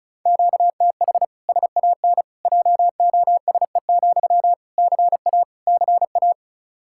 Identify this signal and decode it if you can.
Morse code, heard as a keyed tone, about 36 wpm, 700 Hz. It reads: QTH SAN JOSE, CA CA
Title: Practice QSO - Exchange 1 - Beginning - 35wpm